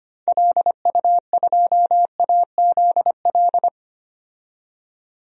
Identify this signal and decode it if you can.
Morse code, heard as a keyed tone, about 25 wpm, 700 Hz. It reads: LU2AZL
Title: DXCC-100 Call Signs - 6 Characters - 25wpm